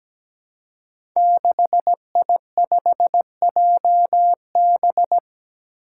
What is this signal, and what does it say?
Morse code, heard as a keyed tone, about 17 wpm, 700 Hz. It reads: 6I5JB
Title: Five Challenging Letter-Number Combinations 17wpm